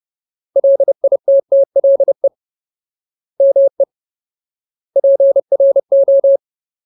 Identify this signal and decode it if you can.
Morse code, heard as a keyed tone, about 30 wpm, 550 Hz. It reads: LITTLE ME PRO
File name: Ultimate Word List / 30WPM Ultimate